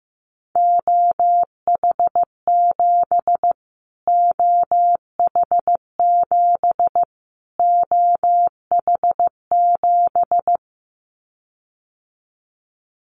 Morse code, heard as a keyed tone, about 15 wpm, 700 Hz. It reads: OH7 OH7 OH7
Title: Callsign Prefixes for Top 25 DXCC - Repeated 3x - 15wpm